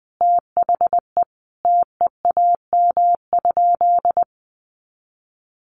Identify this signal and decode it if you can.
Morse code, heard as a keyed tone, about 20 wpm, 700 Hz. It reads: THE TEAM?
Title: Sentences from Top 1500 Words 20wpm